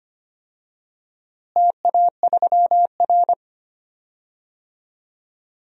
Morse code, heard as a keyed tone, about 25 wpm, 700 Hz. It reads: TA3R
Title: DXCC-100 Call Signs - 4 Characters - Repeated 2x - 25wpm